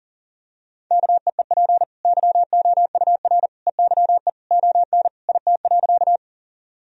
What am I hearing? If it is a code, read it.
Morse code, heard as a keyed tone, about 40 wpm, 700 Hz. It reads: KEEP YOUR EYE ON IT.
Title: Sentences from Top 1000 Words - Review Part 1 - 40wpm